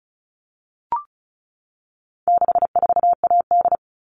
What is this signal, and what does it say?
Morse code, heard as a keyed tone, about 34 wpm, 700 Hz. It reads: 64AD
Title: All Cut Numbers - 4-Digit 35wpm